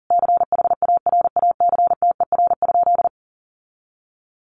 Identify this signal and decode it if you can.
Morse code, heard as a keyed tone, about 40 wpm, 700 Hz. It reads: CHARACTER?
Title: Sentences from Top 1200 Words 40wpm